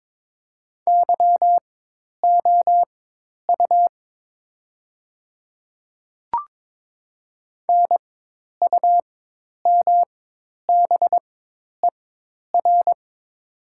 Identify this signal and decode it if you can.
Morse code, heard as a keyed tone, about 22 wpm, spaced as for 10 wpm, 700 Hz. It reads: YOU NUMBER
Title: 4x Character Spacing - Top 100 Words - 22wpm